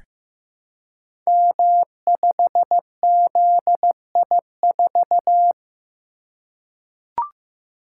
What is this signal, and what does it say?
Morse code, heard as a keyed tone, about 15 wpm, 700 Hz. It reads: M5ZI4
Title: Code Groups - 5 Letter:Numbers - 15wpm